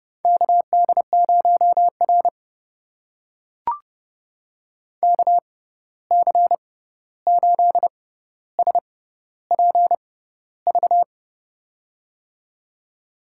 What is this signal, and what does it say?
Morse code, heard as a keyed tone, about 31 wpm, spaced as for 14 wpm, 700 Hz. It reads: KD0R KC8SPV
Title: USA Call Signs - 6x Character Spacing - 30wpm